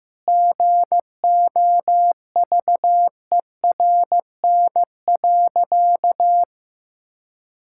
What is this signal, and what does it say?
Morse code, heard as a keyed tone, about 16 wpm, 700 Hz. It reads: GOVERN.
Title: Sentences from Top 1000 Words - Review Part 5 - 15wpm